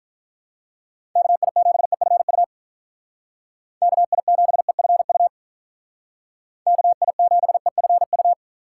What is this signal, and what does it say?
Morse code, heard as a keyed tone, about 43 wpm, 700 Hz. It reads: KI7EFU KI7EFU KI7EFU
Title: Speed Racing - USA 2x3 Call Signs 35wpm